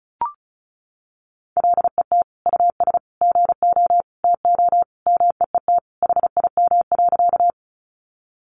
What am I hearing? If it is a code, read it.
Morse code, heard as a keyed tone, about 34 wpm, 700 Hz. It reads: LET US GO TO MEET HIM.
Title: Sentences from Top 1100 Words 35wpm